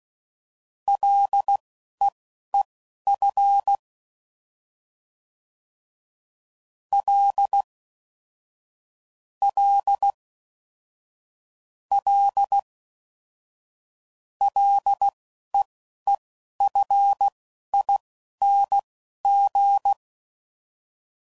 Morse code, heard as a keyed tone, about 16 wpm, spaced as for 11 wpm, 800 Hz. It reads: LEEF L L L LEEFING
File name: Prefixes / FLs & THs / L Prefixes